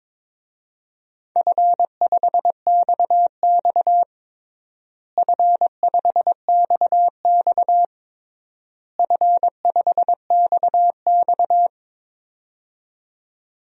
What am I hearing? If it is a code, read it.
Morse code, heard as a keyed tone, about 22 wpm, 700 Hz. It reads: F5XX F5XX F5XX
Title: DXCC-100 Call Signs - 4 Characters - Repeated 3x - 22wpm